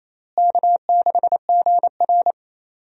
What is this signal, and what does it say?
Morse code, heard as a keyed tone, about 28 wpm, 700 Hz. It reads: K6GR
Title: USA Call Signs - 4 Characters - Repeated 3x - 28wpm